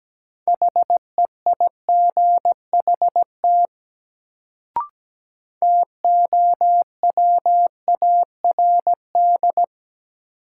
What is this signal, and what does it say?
Morse code, heard as a keyed tone, about 17 wpm, 700 Hz. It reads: HEIGHT TOWARD
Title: Sets of 5 Words - New Word Drill - 17wpm